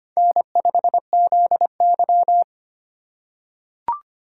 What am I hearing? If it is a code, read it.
Morse code, heard as a keyed tone, about 25 wpm, 700 Hz. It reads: N5ZY
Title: USA Call Signs - 4 Characters - 6x Character Spacing - 25wpm